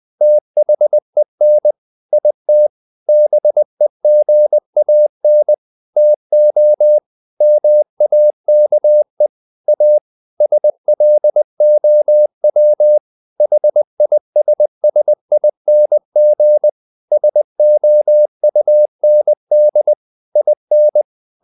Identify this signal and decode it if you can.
Morse code, heard as a keyed tone, about 20 wpm, 600 Hz. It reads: THEN IT BEGAN TO MAKE A SLOW HISSING SOUND IN
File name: the old man and the sea / the old man and the sea part 1 20wpm